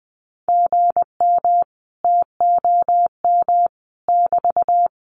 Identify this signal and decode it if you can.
Morse code, heard as a keyed tone, about 20 wpm, 700 Hz. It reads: GM TOM <BT>
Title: Common QSO Phrases Encore 20wpm